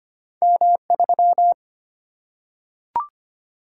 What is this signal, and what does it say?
Morse code, heard as a keyed tone, about 25 wpm, 700 Hz. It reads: M 3
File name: Two Letter-Number Combinations - 25wpm